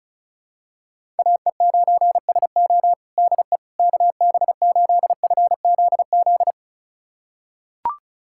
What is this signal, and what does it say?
Morse code, heard as a keyed tone, about 34 wpm, 700 Hz. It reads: AE9SO DE KB8FZZ
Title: USA Callsign DE Callsign 35wpm